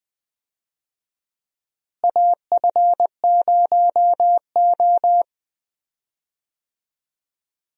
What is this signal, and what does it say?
Morse code, heard as a keyed tone, about 20 wpm, 700 Hz. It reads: AF0O
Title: USA Call Signs - 4 Characters - Repeated 3x - 20wpm